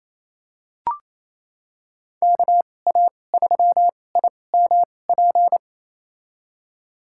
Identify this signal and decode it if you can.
Morse code, heard as a keyed tone, about 28 wpm, spaced as for 20 wpm, 700 Hz. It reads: KA3IMP
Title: USA Call Signs - 6 Characters - 2x Character Spacing - 28wpm